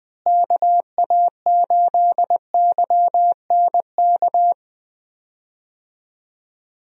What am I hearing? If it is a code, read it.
Morse code, heard as a keyed tone, about 20 wpm, 700 Hz. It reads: KA8YNK
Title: USA Call Signs - 6 Characters - Repeated 3x - 20wpm